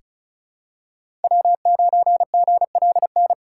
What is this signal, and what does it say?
Morse code, heard as a keyed tone, about 36 wpm, 700 Hz. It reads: W9GLN
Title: USA Call Signs - 5 Characters - 4x Character Spacing - 35wpm